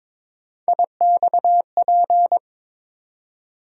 Morse code, heard as a keyed tone, about 22 wpm, 700 Hz. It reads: IXP
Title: USA Call Sign Postfixes - 3 Letters 22wpm